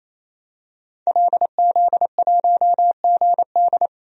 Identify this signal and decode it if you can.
Morse code, heard as a keyed tone, about 28 wpm, 700 Hz. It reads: LZ1GD
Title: DXCC-100 Call Signs - 5 Characters - 28wpm